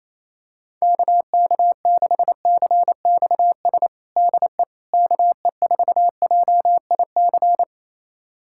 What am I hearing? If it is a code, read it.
Morse code, heard as a keyed tone, about 28 wpm, 700 Hz. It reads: KK6CXS DE KE4JIC
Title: USA Callsign DE Callsign - 6 Characters - 28wpm